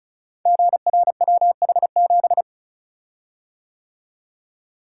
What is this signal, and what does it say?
Morse code, heard as a keyed tone, about 36 wpm, 700 Hz. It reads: GRWH7
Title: Code Groups - 5 Letter:Numbers - 35wpm